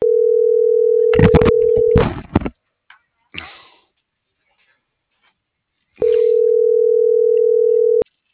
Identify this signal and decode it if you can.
Ringback tone (North America), 440 + 480 Hz.